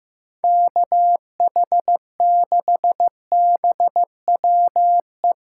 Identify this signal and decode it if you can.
Morse code, heard as a keyed tone, about 15 wpm, 700 Hz. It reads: KH6BWE
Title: USA Callsign DE Callsign 15wpm